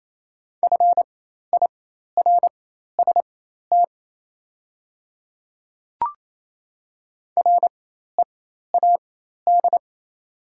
Morse code, heard as a keyed tone, about 28 wpm, spaced as for 13 wpm, 690 Hz. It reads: FIRST READ
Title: 4x Character Spacing - Top 200 Words - 28wpm